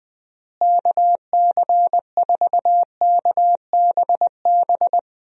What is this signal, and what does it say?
Morse code, heard as a keyed tone, about 20 wpm, 700 Hz. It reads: KC4KBB
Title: USA Call Signs - 6 Characters - Repeated 3x - 20wpm